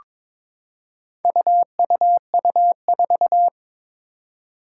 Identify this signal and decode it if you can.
Morse code, heard as a keyed tone, about 22 wpm, 700 Hz. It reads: UUU4
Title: Four Challenging Letter-Number Combinations 22wpm